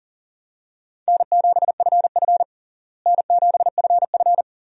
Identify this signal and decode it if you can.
Morse code, heard as a keyed tone, about 37 wpm, 700 Hz. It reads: N7FF N7FF
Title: USA Call Signs - 4 Characters - Repeated 2x - 40wpm